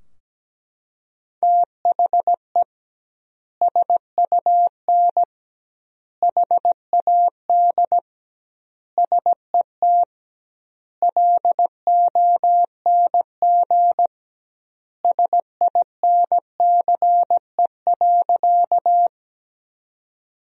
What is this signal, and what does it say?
Morse code, heard as a keyed tone, about 17 wpm, 700 Hz. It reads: THE SUN HAD SET LONG SINCE.
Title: 2x Word-Spacing - Sentences from Top 300 Words - 17wpm